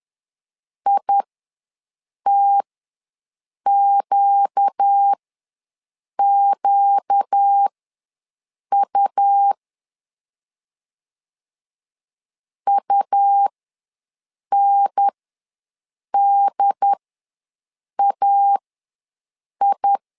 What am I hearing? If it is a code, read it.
Morse code, heard as a keyed tone, about 10 wpm, spaced as for 6 wpm, 800 Hz. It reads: ITQQU UNDAI
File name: LEZIONI / 4 Q ws 10wpm 2306